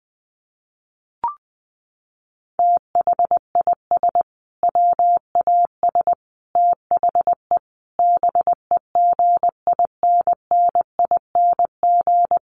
Morse code, heard as a keyed tone, about 20 wpm, 700 Hz. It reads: THIS WAS THE BEGINNING
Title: Sentences from Top 900 Words 20wpm